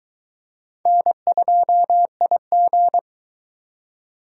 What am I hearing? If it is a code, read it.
Morse code, heard as a keyed tone, about 22 wpm, 700 Hz. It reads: N2IG